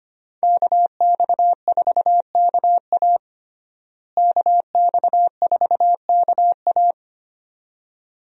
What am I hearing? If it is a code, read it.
Morse code, heard as a keyed tone, about 25 wpm, 700 Hz. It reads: KX4KA KX4KA
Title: USA Call Signs - 5 Characters - Repeated 2x - 25wpm